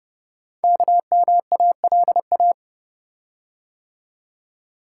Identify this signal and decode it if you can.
Morse code, heard as a keyed tone, about 30 wpm, 700 Hz. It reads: KMALA
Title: Code Groups - 5 Letters - 30wpm